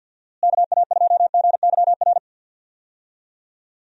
Morse code, heard as a keyed tone, about 48 wpm, 700 Hz. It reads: KA2GYR
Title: USA Call Signs - 6 Characters - Repeated 3x - 50wpm